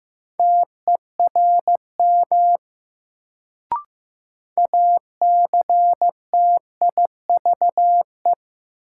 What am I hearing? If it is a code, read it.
Morse code, heard as a keyed tone, about 16 wpm, 700 Hz. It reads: TERM ACTIVE
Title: Sentences from Top 900 Words - New Word Drill - 15wpm